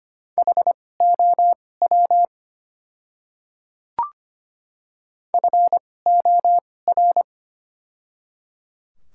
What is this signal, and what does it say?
Morse code, heard as a keyed tone, about 25 wpm, spaced as for 18 wpm, 700 Hz. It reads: HOW FOR
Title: 2x Character Spacing - Top 100 Words - 25wpm